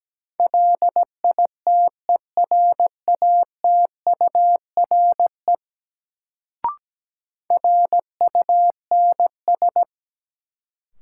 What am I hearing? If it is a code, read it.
Morse code, heard as a keyed tone, about 17 wpm, 700 Hz. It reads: LITERATURE RUNS
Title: Sentences from Top 600 Words - New Word Drill - 17wpm